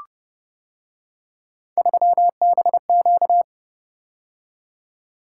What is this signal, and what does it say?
Morse code, heard as a keyed tone, about 29 wpm, 700 Hz. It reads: 3BQ